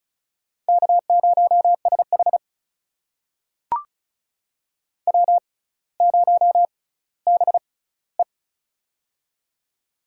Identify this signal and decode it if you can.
Morse code, heard as a keyed tone, about 36 wpm, spaced as for 18 wpm, 700 Hz. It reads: K0SH W0BE
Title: USA Call Signs - 4 Characters - 6x Character Spacing - 35wpm